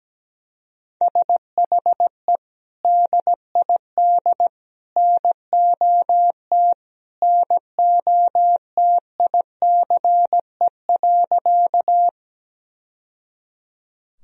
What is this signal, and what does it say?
Morse code, heard as a keyed tone, about 17 wpm, 700 Hz. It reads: SHE DID NOT NOTICE.
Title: Sentences from Top 1000 Words - Review Part 2 - 17wpm